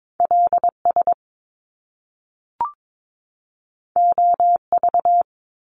Morse code, heard as a keyed tone, about 22 wpm, 700 Hz. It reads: L S O V
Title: Top Two Letter Combinations in Words 22wpm